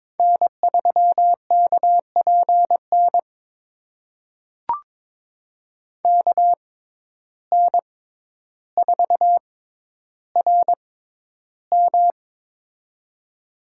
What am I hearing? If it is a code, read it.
Morse code, heard as a keyed tone, about 22 wpm, spaced as for 11 wpm, 700 Hz. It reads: N3KPN KN4RM
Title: USA Call Signs - 5 Characters - 6x Character Spacing - 22wpm